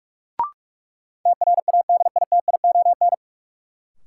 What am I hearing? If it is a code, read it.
Morse code, heard as a keyed tone, about 46 wpm, 700 Hz. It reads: TRADITION
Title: Sentences from Top 1500 Words - New Word Drill Alt - 45wpm